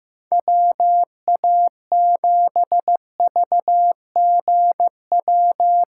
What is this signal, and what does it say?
Morse code, heard as a keyed tone, about 15 wpm, 700 Hz. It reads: WA7VGW
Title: USA Call Signs - 6 Characters - 8x Character Spacing - 15wpm